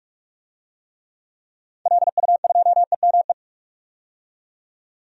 Morse code, heard as a keyed tone, about 41 wpm, 700 Hz. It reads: LU2EME
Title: DXCC-100 Call Signs - 6 Characters - 45wpm